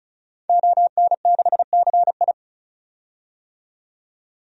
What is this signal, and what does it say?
Morse code, heard as a keyed tone, about 34 wpm, 700 Hz. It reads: ON6CI
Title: DXCC-100 Call Signs - 5 Characters - 35wpm